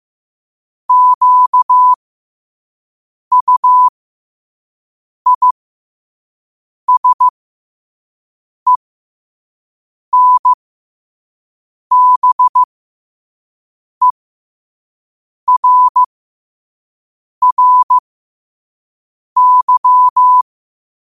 Morse code, heard as a keyed tone, about 15 wpm, spaced as for 5 wpm, 1000 Hz. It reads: QUISENBERRY
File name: qotdfiles / 5wpm